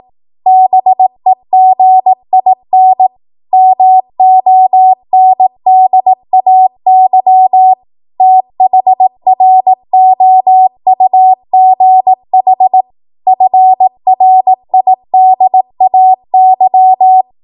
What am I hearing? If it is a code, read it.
Morse code, heard as a keyed tone, about 18 wpm, 750 Hz. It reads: BEGIN MONDAY THROUGH FRIDAY
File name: Archive / 18 WPM / 18WPM